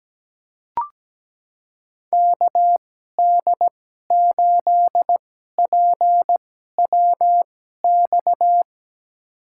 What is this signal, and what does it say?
Morse code, heard as a keyed tone, about 17 wpm, spaced as for 12 wpm, 700 Hz. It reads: KD8PWX
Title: USA Call Signs - 2x Character Spacing - 17wpm